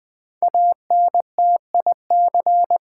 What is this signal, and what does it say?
Morse code, heard as a keyed tone, about 20 wpm, 700 Hz. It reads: ANTIC
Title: Build A Word - 5 Letters At A Time - 10-Letter Words - 20wpm